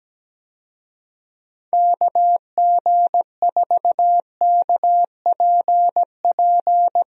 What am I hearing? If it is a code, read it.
Morse code, heard as a keyed tone, about 17 wpm, 700 Hz. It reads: KG4KPP
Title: USA Call Signs - 6 Characters - Repeated 3x - 17wpm